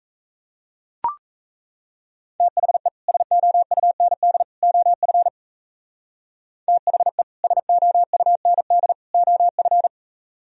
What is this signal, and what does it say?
Morse code, heard as a keyed tone, about 33 wpm, 700 Hz. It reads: THE SOUND OF THE SOUND OF
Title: Speed-Racing - Sets of 3 Words 28wpm